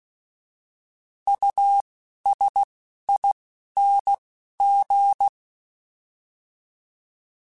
Morse code, heard as a keyed tone, about 16 wpm, spaced as for 11 wpm, 800 Hz. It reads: USING
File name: Alphabetical / Alphabetical mp3s / Us @10